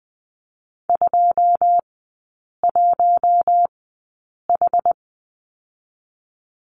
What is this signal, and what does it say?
Morse code, heard as a keyed tone, about 20 wpm, spaced as for 8 wpm, 700 Hz. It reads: 21H
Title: 081 Lesson 14e - Number 2 - Three Characters 20wpm